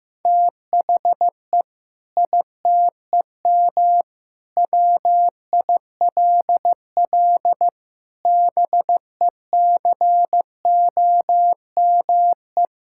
Morse code, heard as a keyed tone, about 15 wpm, 700 Hz. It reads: THE ITEM WILL BECOME
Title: Sentences from Top 1000 Words - Review Part 6 - 15wpm